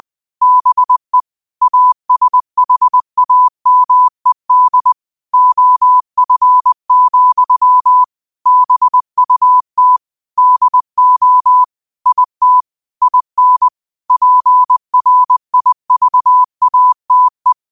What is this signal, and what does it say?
Morse code, heard as a keyed tone, about 20 wpm, 1000 Hz. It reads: BE ASHAMED OF, BUT DO IT IN PRIVATE
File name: 20wpm